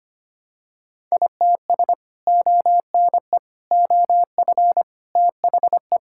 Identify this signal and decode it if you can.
Morse code, heard as a keyed tone, about 25 wpm, 700 Hz. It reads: ITS ONE OF THE
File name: practice / 25wpm / Sentences with 3-Word Phrases - 25wpm